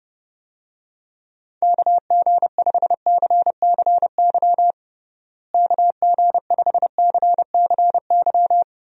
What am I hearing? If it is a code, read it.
Morse code, heard as a keyed tone, about 30 wpm, 700 Hz. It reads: KG5CCY KG5CCY
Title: USA Call Signs - 6 Characters - Repeated 2x - 30wpm